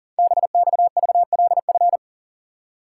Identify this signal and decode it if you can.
Morse code, heard as a keyed tone, about 40 wpm, 700 Hz. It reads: BXVLF
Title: Code Groups - 5 Letters - 40wpm